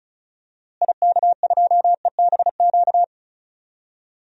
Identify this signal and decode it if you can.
Morse code, heard as a keyed tone, about 34 wpm, 700 Hz. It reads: IK2EBQ
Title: DXCC-100 Call Signs - 6 Characters - Repeated 3x - 35wpm